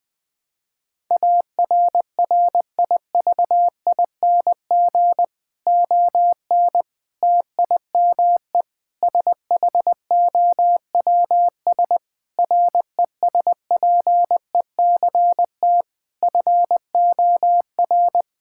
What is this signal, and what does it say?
Morse code, heard as a keyed tone, about 20 wpm, 700 Hz. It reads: ARRIVING ON TIME SHOWS RESPECT FOR
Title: Difficult Sentences with Hints - 20wpm